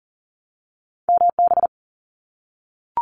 Morse code, heard as a keyed tone, about 38 wpm, 700 Hz. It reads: MB
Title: Canadian Provinces and Territories 40wpm